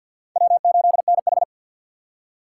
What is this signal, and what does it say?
Morse code, heard as a keyed tone, about 46 wpm, 700 Hz. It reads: W8NH
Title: USA Call Signs - 4 Characters - 2x Character Spacing - 50wpm